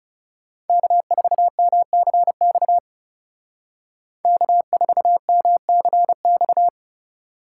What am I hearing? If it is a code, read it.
Morse code, heard as a keyed tone, about 35 wpm, 700 Hz. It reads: K4MCX K4MCX
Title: Speed Racing - USA 1x3 Call Signs 30wpm